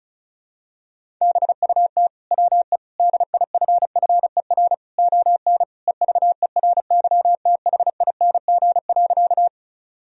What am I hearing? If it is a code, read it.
Morse code, heard as a keyed tone, about 36 wpm, 700 Hz. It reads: BUT WE DIFFER ON EVERYTHING.
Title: Sentences from Top 1000 Words - Review Part 1 - 35wpm